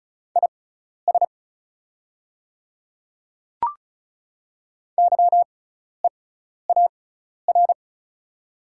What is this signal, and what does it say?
Morse code, heard as a keyed tone, about 36 wpm, spaced as for 12 wpm, 690 Hz. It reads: IS YEAR
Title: 6x Character Spacing - Top 300 Words - 35wpm